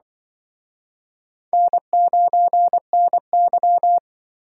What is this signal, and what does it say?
Morse code, heard as a keyed tone, about 24 wpm, 700 Hz. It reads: N9NY
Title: Speed Racing - USA Call Signs 20wpm